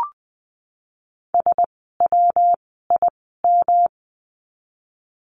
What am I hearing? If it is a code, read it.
Morse code, heard as a keyed tone, about 20 wpm, spaced as for 15 wpm, 700 Hz. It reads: SWIM
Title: 115 Lesson 19f - Letter w - Words 20wpm